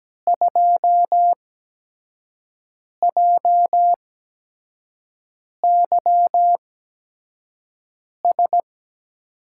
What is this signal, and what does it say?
Morse code, heard as a keyed tone, about 17 wpm, spaced as for 5 wpm, 700 Hz. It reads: 2JYS
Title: USA Call Signs - 6 Characters - 8x Character Spacing - 17wpm